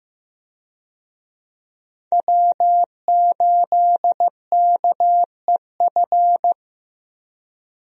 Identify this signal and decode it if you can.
Morse code, heard as a keyed tone, about 15 wpm, 700 Hz. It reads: W8KEF
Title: USA Call Signs - 5 Characters - Repeated 2x - 15wpm